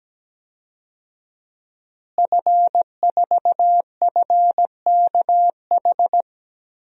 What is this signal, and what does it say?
Morse code, heard as a keyed tone, about 17 wpm, 700 Hz. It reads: F4FKH
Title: DXCC-100 Call Signs - 5 Characters - Repeated 3x - 17wpm